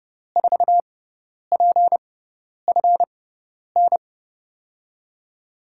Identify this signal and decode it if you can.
Morse code, heard as a keyed tone, about 30 wpm, spaced as for 10 wpm, 700 Hz. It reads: 4PFN
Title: USA Call Signs - 6 Characters - 6x Character Spacing - 30wpm